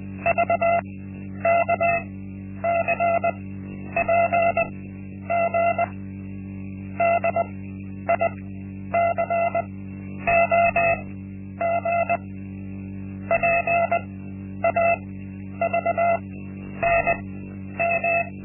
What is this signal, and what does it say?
Morse code, heard as a keyed tone, about 20 wpm, spaced as for 10 wpm, 680 Hz. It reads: VKCPG DICOG PAVNM
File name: Linksmakalnis-1